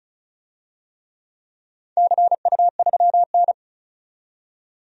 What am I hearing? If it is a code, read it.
Morse code, heard as a keyed tone, about 36 wpm, 700 Hz. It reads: CU3N
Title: DXCC-100 Call Signs - 4 Characters - 35wpm